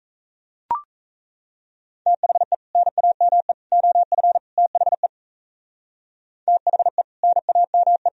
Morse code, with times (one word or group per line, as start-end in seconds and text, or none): 2.1-2.5 THE
2.8-3.5 NAME
3.7-4.4 OF
4.6-5.1 THE
6.5-7.0 THE
7.2-8.1 NAME